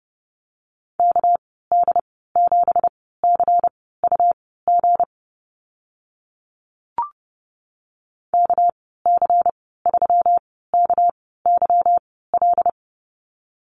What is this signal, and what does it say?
Morse code, heard as a keyed tone, about 30 wpm, spaced as for 17 wpm, 700 Hz. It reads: KD7CUG KC3KYL
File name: Lesson 37g - Letter x - Call Signs 30wpm